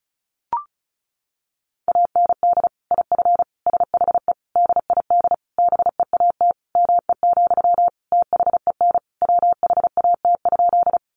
Morse code, heard as a keyed tone, about 34 wpm, 700 Hz. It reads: AND IF SHE DID BEAT ME, THEN WHAT?
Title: Sentences from Top 1300 Words 35wpm